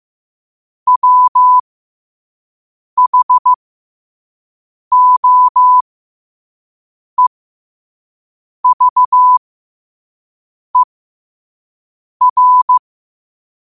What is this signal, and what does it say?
Morse code, heard as a keyed tone, about 15 wpm, spaced as for 5 wpm, 1000 Hz. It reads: WHOEVER
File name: 5wpm